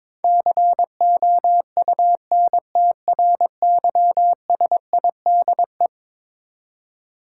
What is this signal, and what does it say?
Morse code, heard as a keyed tone, about 22 wpm, 700 Hz. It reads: COUNTRYSIDE
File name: practice / 22wpm / Eleven Letter Words - 22wpm